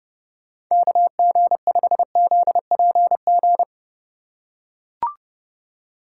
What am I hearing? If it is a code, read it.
Morse code, heard as a keyed tone, about 29 wpm, 700 Hz. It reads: KG5ZPG